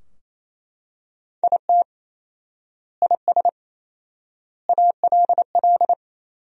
Morse code, heard as a keyed tone, about 28 wpm, 700 Hz. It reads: IT IS ALL
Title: 4x Word-Spacing - Sentences from Top 500 Words - 28wpm